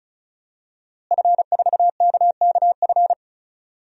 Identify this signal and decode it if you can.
Morse code, heard as a keyed tone, about 36 wpm, 700 Hz. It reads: F4KKF
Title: DXCC-100 Call Signs - 5 Characters - Repeated 2x - 35wpm